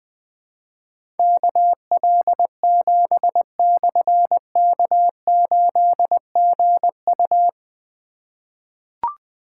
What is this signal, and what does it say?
Morse code, heard as a keyed tone, about 20 wpm, 700 Hz. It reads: KL7/K8GU
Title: Difficult Call Signs 20wpm